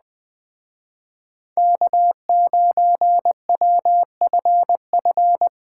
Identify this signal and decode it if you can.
Morse code, heard as a keyed tone, about 20 wpm, 700 Hz. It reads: K9WFF